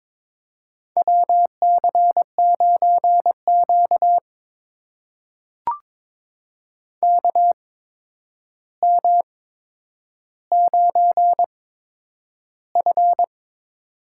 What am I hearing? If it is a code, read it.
Morse code, heard as a keyed tone, about 22 wpm, spaced as for 9 wpm, 700 Hz. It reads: WC9Q KM9F
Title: USA Call Signs - 4 Characters - 8x Character Spacing - 22wpm